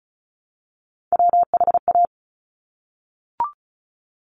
Morse code, heard as a keyed tone, about 36 wpm, 700 Hz. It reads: WHA
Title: Top Three Letter Combinations in Words 35wpm